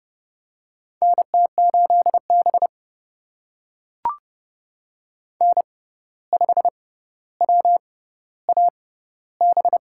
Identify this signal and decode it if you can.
Morse code, heard as a keyed tone, about 30 wpm, spaced as for 14 wpm, 700 Hz. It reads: NT8B N5WAB
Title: USA Call Signs - 6x Character Spacing - 30wpm